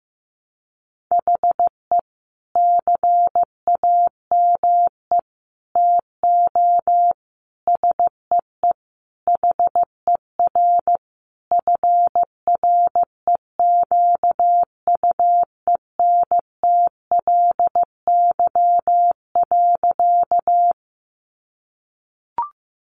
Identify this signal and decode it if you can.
Morse code, heard as a keyed tone, about 15 wpm, 700 Hz. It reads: HE CAME TO SEE HER FREQUENTLY.
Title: Sentences from Top 1300 Words 15wpm